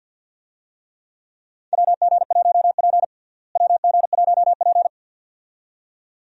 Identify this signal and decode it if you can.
Morse code, heard as a keyed tone, about 47 wpm, 700 Hz. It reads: WG1P WG1P
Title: USA Call Signs - 4 Characters - Repeated 2x - 50wpm